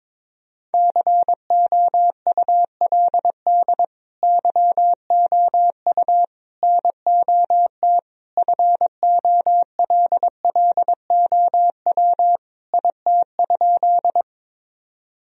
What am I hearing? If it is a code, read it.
Morse code, heard as a keyed tone, about 21 wpm, 700 Hz. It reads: COULD YOU NOT FOLLOW IT?
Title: Sentences from Top 1000 Words - Review Part 1 - 22wpm